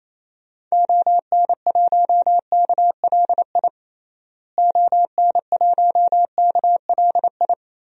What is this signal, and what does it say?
Morse code, heard as a keyed tone, about 28 wpm, 700 Hz. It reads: ON1KLI ON1KLI
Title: DXCC-100 Call Signs - 6 Characters - Repeated 3x - 28wpm